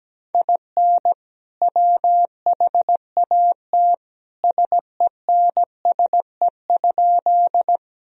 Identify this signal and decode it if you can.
Morse code, heard as a keyed tone, about 17 wpm, 700 Hz. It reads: IN WHAT SENSE?
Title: Sentences from Top 1000 Words - Review Part 4 - 17wpm